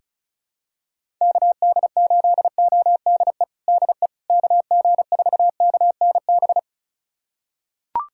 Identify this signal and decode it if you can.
Morse code, heard as a keyed tone, about 34 wpm, 700 Hz. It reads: KD8ODE DE KG4KNB
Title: USA Callsign DE Callsign - 6 Characters - 35wpm